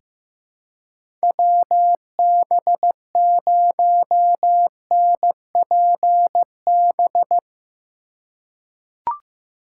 Morse code, heard as a keyed tone, about 16 wpm, 700 Hz. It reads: WB0NPB
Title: USA Call Signs - 2x Character Spacing - 15wpm